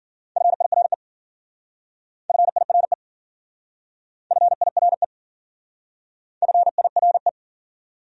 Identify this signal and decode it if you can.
Morse code, heard as a keyed tone, about 50 wpm, 700 Hz. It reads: FIRE FIRE FIRE FIRE